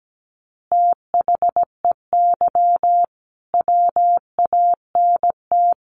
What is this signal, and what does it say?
Morse code, heard as a keyed tone, about 17 wpm, 700 Hz. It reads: THEY WANT
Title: Easy Phrases 17wpm